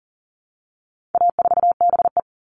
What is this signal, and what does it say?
Morse code, heard as a keyed tone, about 40 wpm, 700 Hz. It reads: A4BE
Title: All Cut Numbers - 4-Digit 40wpm